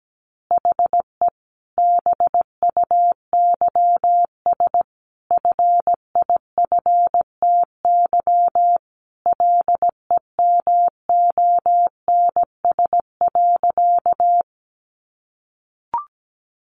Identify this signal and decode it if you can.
Morse code, heard as a keyed tone, about 17 wpm, 700 Hz. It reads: HE BUYS FIFTY LEMONS.
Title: Additional Easy Phrases 17wpm